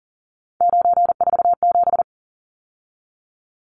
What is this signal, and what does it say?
Morse code, heard as a keyed tone, about 40 wpm, 700 Hz. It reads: 947
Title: Numbers - 3-Digit 40wpm